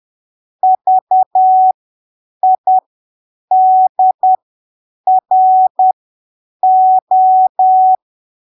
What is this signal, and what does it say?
Morse code, heard as a keyed tone, about 10 wpm, spaced as for 7 wpm, 750 Hz. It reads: VIDRO
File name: arquivos / texto / texto 11 10ppm